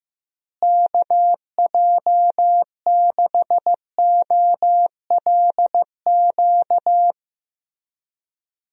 Morse code, heard as a keyed tone, about 15 wpm, 700 Hz. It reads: KJ6OLQ
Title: USA Call Signs - 6 Characters - Repeated 2x - 15wpm